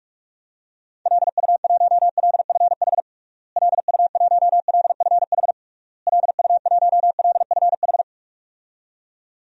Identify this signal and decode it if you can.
Morse code, heard as a keyed tone, about 41 wpm, 700 Hz. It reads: LU1LFH LU1LFH LU1LFH
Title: DXCC-100 Call Signs - 6 Characters - Repeated 3x - 45wpm